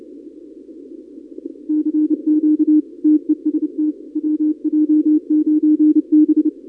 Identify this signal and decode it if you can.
Morse code, heard as a keyed tone, about 30 wpm, 300 Hz. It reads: CQ TEST WJ9B
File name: contests / 2019-cw-wpx-kt5j-k5pi- -2330 0-31